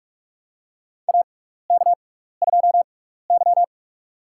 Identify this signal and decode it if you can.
Morse code, heard as a keyed tone, about 43 wpm, spaced as for 15 wpm, 700 Hz. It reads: AK2Y
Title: USA Call Signs - 4 Characters - 6x Character Spacing - 45wpm